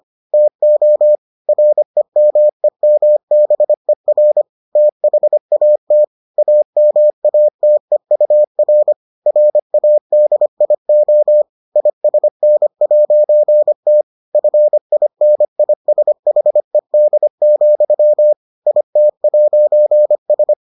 Morse code, heard as a keyed tone, about 25 wpm, 600 Hz. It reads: TO REMEMBER THAT AMATEUR RADIO ISN'T FINISHED, IT'S